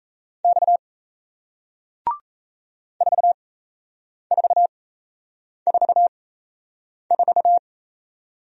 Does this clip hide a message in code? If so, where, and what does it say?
Yes: Morse code from 0.4 to 7.6 s, heard as a keyed tone, about 38 wpm, spaced as for 8 wpm, 700 Hz. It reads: X 4444